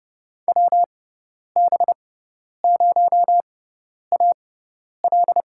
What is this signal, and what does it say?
Morse code, heard as a keyed tone, about 29 wpm, spaced as for 10 wpm, 700 Hz. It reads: WB0AL
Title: USA Call Signs - 5 Characters - 6x Character Spacing - 30wpm